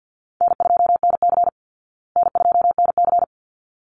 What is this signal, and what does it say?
Morse code, heard as a keyed tone, about 49 wpm, 700 Hz. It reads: N2NC N2NC
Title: Call Signs Repeated 3x - 50wpm